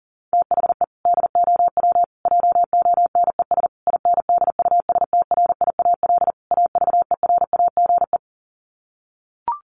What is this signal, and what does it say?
Morse code, heard as a keyed tone, about 40 wpm, 700 Hz. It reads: THE DOW JONES INDUSTRIAL AVERAGE
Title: Sets of 5 Words 40wpm